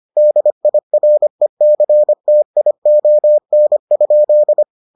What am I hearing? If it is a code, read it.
Morse code, heard as a keyed tone, about 25 wpm, 600 Hz. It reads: DIRECTION?